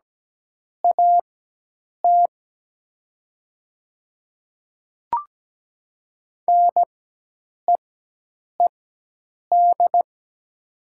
Morse code, heard as a keyed tone, about 17 wpm, spaced as for 8 wpm, 700 Hz. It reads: AT NEED
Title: 4x Character Spacing - Top 300 Words - 17wpm